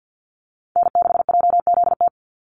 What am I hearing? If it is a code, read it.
Morse code, heard as a keyed tone, about 50 wpm, 700 Hz. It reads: N6JZT
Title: NA SOTA Activator Callsigns 50wpm